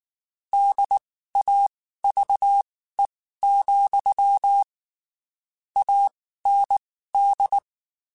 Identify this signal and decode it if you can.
Morse code, heard as a keyed tone, about 19 wpm, spaced as for 14 wpm, 800 Hz. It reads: DAVE, AND
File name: (spaced) / QSOs Spaced-mp3s / kc7haa @18 sp